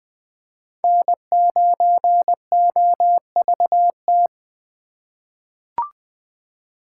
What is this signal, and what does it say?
Morse code, heard as a keyed tone, about 20 wpm, 700 Hz. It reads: N9OVT
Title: USA Call Signs - 5 Characters - 3x Character Spacing - 20wpm